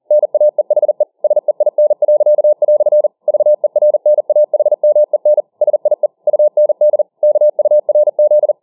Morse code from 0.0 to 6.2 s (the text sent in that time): NAEHE SEIN." VERNAHMEN SIE